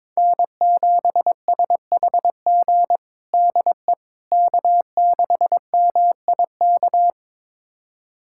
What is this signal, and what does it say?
Morse code, heard as a keyed tone, about 21 wpm, 700 Hz. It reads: N7SHG DE K6MIK
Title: USA Callsign DE Callsign - 5 Characters - 22wpm